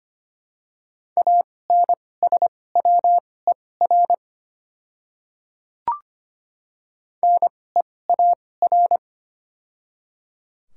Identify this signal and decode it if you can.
Morse code, heard as a keyed tone, about 25 wpm, spaced as for 18 wpm, 690 Hz. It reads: ANSWER NEAR
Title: 2x Character Spacing - Top 300 Words - 25wpm